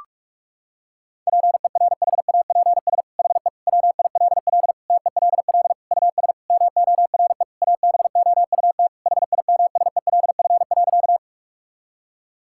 Morse code, heard as a keyed tone, about 46 wpm, 700 Hz. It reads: PERHAPS HE WILL TELL US MORE ABOUT HIMSELF.
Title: Sentences from Top 1600 Words 45wpm